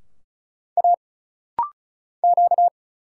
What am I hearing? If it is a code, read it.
Morse code, heard as a keyed tone, about 36 wpm, 700 Hz. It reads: A Q
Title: Single Letter-Number Alt - Rapid-Fire 35wpm